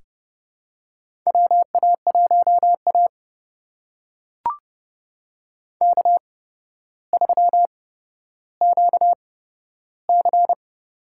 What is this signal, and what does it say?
Morse code, heard as a keyed tone, about 32 wpm, spaced as for 13 wpm, 700 Hz. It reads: WA1A K3QC
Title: USA Call Signs - 4 Characters - 8x Character Spacing - 30wpm